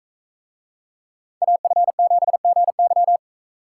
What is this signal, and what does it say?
Morse code, heard as a keyed tone, about 41 wpm, 700 Hz. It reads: AF7GY